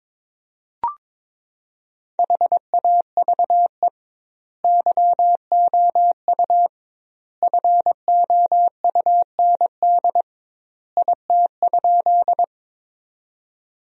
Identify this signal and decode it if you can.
Morse code, heard as a keyed tone, about 22 wpm, 700 Hz. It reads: HAVE YOU FOUND IT?
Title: 2x Word-Spacing - Sentences from Top 300 Words - 22wpm